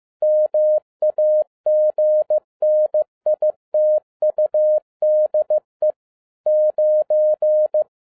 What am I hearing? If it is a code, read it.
Morse code, heard as a keyed tone, about 15 wpm, 610 Hz. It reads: MAGNITUDE 9
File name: 2011-10-mp3 / n02-FIELD DAY CONTEST QRV FROM SAKAE VILLAGE 15wpm